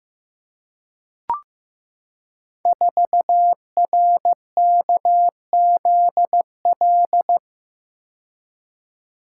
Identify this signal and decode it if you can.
Morse code, heard as a keyed tone, about 15 wpm, 700 Hz. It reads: 4RKZL
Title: Code Groups - 5 Letter:Numbers - 15wpm